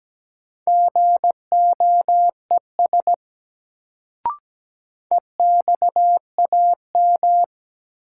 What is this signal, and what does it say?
Morse code, heard as a keyed tone, about 17 wpm, 700 Hz. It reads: GOES EXAM
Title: Sentences from Top 1000 Words - Review Part 5 - New Word Drill - 17wpm